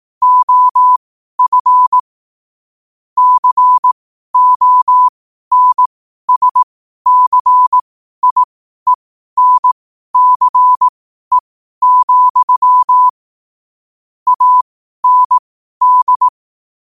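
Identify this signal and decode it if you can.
Morse code, heard as a keyed tone, about 18 wpm, spaced as for 12 wpm, 1000 Hz. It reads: OF CONSCIENCE, AND
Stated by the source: 12 wpm Morse code.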